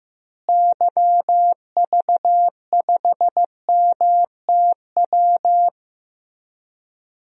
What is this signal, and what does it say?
Morse code, heard as a keyed tone, about 15 wpm, 700 Hz. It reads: YV5MTW
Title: DXCC-100 Call Signs - 6 Characters - Repeated 2x - 15wpm